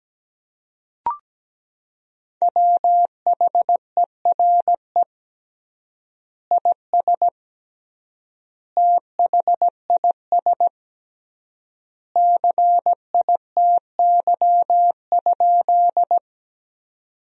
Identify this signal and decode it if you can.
Morse code, heard as a keyed tone, about 17 wpm, 700 Hz. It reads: WHERE IS THIS CITY?
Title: 3x Word-Spacing - Sentences from Top 300 Words - 17wpm